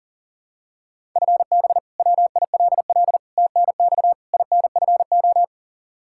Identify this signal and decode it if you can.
Morse code, heard as a keyed tone, about 37 wpm, 700 Hz. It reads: FB WILL TNX INFO
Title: Practice QSO - Exchange 3 - More Info - 40wpm